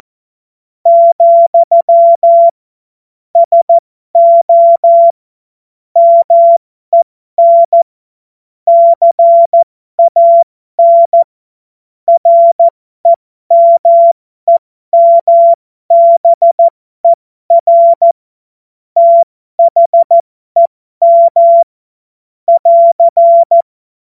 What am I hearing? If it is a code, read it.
Morse code, heard as a keyed tone, about 14 wpm, spaced as for 12 wpm, 680 Hz. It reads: , SO MEN CAN REMEMBER THEM <AR>
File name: Morsekurs / Witze